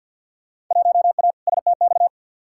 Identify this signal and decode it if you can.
Morse code, heard as a keyed tone, about 52 wpm, 700 Hz. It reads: 1A STX
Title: Field Day Exchanges Encore - 50wpm